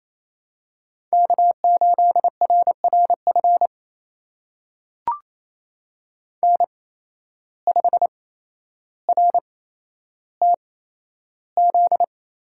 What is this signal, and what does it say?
Morse code, heard as a keyed tone, about 28 wpm, spaced as for 12 wpm, 700 Hz. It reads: K8RRF N5RTZ
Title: USA Call Signs - 5 Characters - 8x Character Spacing - 28wpm